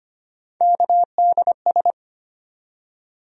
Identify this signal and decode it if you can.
Morse code, heard as a keyed tone, about 25 wpm, 700 Hz. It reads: KDS
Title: USA Call Sign Postfixes - 3 Letters 25wpm